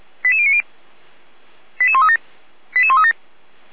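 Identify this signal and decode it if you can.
Five-tone selective call (ZVEI), heard as quick runs of five tones, tones of about 70 ms. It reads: 88090 88137 88137
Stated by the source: ZVEI five-tone call received on SDR, fire brigade Orlamünde (feuerwehr-orlamuende) server.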